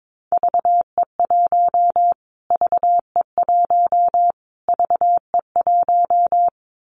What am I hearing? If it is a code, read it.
Morse code, heard as a keyed tone, about 22 wpm, 700 Hz. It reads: VE1 VE1 VE1
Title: Callsign Prefixes for Top 25 DXCC - Repeated 3x - 22wpm